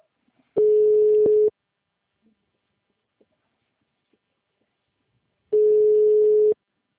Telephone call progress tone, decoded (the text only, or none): ringback tone